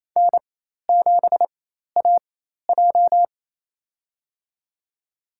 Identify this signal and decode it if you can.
Morse code, heard as a keyed tone, about 28 wpm, spaced as for 13 wpm, 700 Hz. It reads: N7AJ
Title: USA Call Signs - 4 Characters - 4x Character Spacing - 28wpm